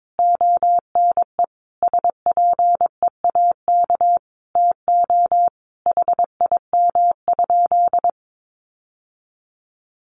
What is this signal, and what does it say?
Morse code, heard as a keyed tone, about 22 wpm, 700 Hz. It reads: ONE SPEAK TO HIM?
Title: Sentences from Top 1100 Words 22wpm